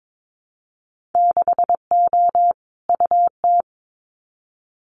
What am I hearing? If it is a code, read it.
Morse code, heard as a keyed tone, about 22 wpm, 700 Hz. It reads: 6O UT
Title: Winter Field Day Exchanges Encore 22wpm